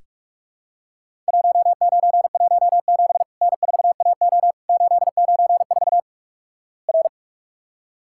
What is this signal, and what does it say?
Morse code, heard as a keyed tone, about 41 wpm, 700 Hz. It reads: 1917 N4AO 894 R
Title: Worked All Europe DX Contest - QTC Exchange - 45wpm